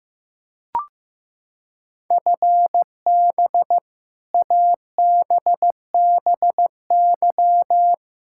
Morse code, heard as a keyed tone, about 15 wpm, 700 Hz. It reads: FB ABBY
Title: Practice QSO - Exchange 3 - More Info - 15wpm